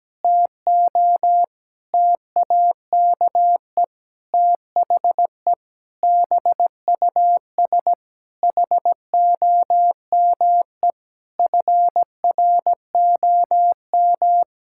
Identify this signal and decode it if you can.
Morse code, heard as a keyed tone, about 17 wpm, 700 Hz. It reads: TO TAKE THE BUS HOME FROM
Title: Sentences from Top 1000 Words - Review Part 6 - 17wpm